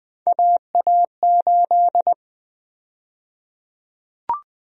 Morse code, heard as a keyed tone, about 20 wpm, 700 Hz. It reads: AA8
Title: USA Call Sign Prefixes - Repeated 2x - 20wpm